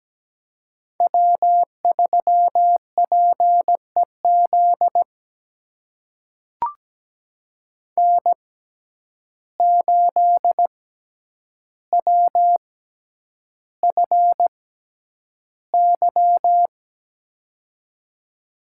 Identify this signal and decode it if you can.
Morse code, heard as a keyed tone, about 17 wpm, spaced as for 9 wpm, 700 Hz. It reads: W3PEZ N8WFY
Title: USA Call Signs - 5 Characters - 6x Character Spacing - 17wpm